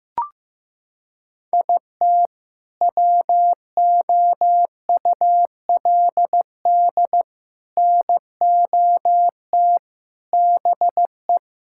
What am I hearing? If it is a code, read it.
Morse code, heard as a keyed tone, about 15 wpm, 700 Hz. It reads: IT WOULD NOT BE
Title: Sentences from Top 1600 Words 15wpm